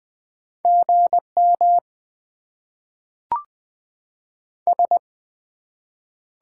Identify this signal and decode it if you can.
Morse code, heard as a keyed tone, about 20 wpm, 700 Hz. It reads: GM S
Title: Top 90 QSO Elements - 8x Character Spacing - 20wpm